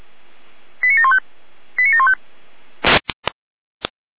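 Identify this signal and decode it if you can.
Five-tone selective call (ZVEI), heard as quick runs of five tones, tones of about 70 ms. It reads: 89715 89715
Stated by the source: ZVEI five-tone selective call received on an SDR.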